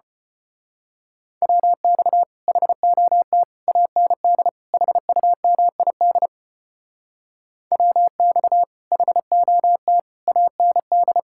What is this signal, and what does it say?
Morse code, heard as a keyed tone, about 29 wpm, 700 Hz. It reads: WX HOT AND HUMID WX HOT AND
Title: Speed-Racing Common QSO Phrases - 25wpm